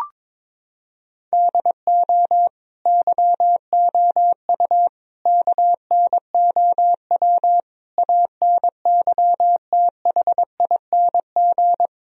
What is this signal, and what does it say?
Morse code, heard as a keyed tone, about 22 wpm, 700 Hz. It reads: DO YOU KNOW ANYTHING
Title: Sentences from Top 1000 Words - Review Part 1 - 22wpm